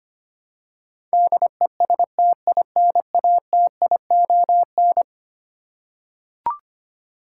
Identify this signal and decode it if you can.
Morse code, heard as a keyed tone, about 25 wpm, 700 Hz. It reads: DESTINATION
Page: morse-code-ninja-practice-25wpm